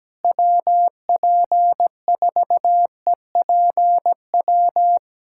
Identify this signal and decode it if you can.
Morse code, heard as a keyed tone, about 17 wpm, 700 Hz. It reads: WP4EPW
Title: USA Callsign DE Callsign - 6 Characters - 17wpm